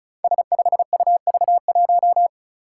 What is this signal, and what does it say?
Morse code, heard as a keyed tone, about 36 wpm, 700 Hz. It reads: S5UV1
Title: Five Challenging Letter-Number Combinations 35wpm